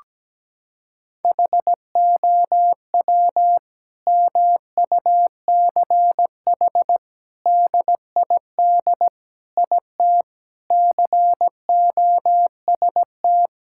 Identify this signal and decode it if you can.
Morse code, heard as a keyed tone, about 17 wpm, 700 Hz. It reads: HOW MUCH DID IT COST
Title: Sentences from Top 1000 Words - Review Part 5 - 17wpm